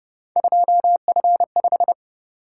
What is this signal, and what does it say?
Morse code, heard as a keyed tone, about 29 wpm, 700 Hz. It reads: 2F5
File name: Speed-Racing - Three Letter-Number Combinations - 25wpm